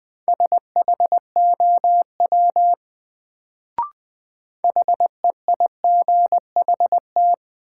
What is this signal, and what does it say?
Morse code, heard as a keyed tone, about 20 wpm, 700 Hz. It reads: SHOW HEIGHT
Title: Sets of 5 Words - New Word Drill - 20wpm